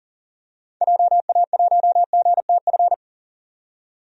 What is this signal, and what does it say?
Morse code, heard as a keyed tone, about 40 wpm, 700 Hz. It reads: JA1GTF
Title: DXCC-100 Call Signs - 6 Characters - 40wpm